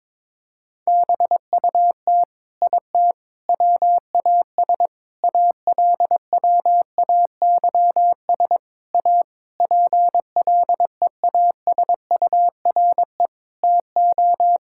Morse code, heard as a keyed tone, about 22 wpm, 700 Hz. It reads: BUT IT WAS ALWAYS A PLEASURE TO
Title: Sentences from Top 1600 Words 22wpm